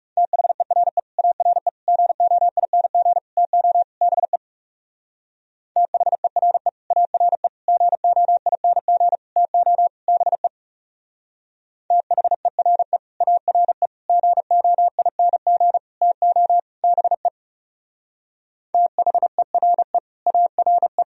Morse code, heard as a keyed tone, about 36 wpm, 700 Hz. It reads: THERE ARE GOING TO BE THERE ARE GOING TO BE THERE ARE GOING TO BE THERE ARE
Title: Speed-Racing - Sets of 5 Words 30wpm